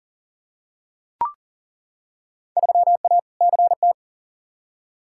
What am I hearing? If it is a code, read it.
Morse code, heard as a keyed tone, about 40 wpm, 700 Hz. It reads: 3A CT
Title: Field Day Exchanges - 40wpm